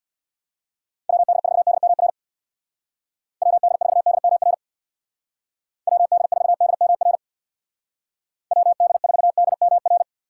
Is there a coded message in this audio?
Yes: Morse code from 1.1 to 10.0 s, heard as a keyed tone, about 64 wpm, 700 Hz. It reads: WD4DMR WD4DMR WD4DMR WD4DMR